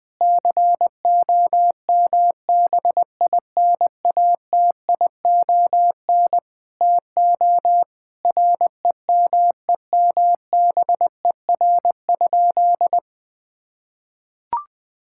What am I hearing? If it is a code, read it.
Morse code, heard as a keyed tone, about 21 wpm, 700 Hz. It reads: COMBINATION TO REMEMBER?
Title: Sentences from Top 1000 Words - Review Part 6 - 20wpm